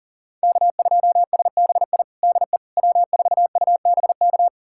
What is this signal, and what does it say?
Morse code, heard as a keyed tone, about 40 wpm, 700 Hz. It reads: K2SBI DE W4UBK
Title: USA Callsign DE Callsign - 5 Characters - 40wpm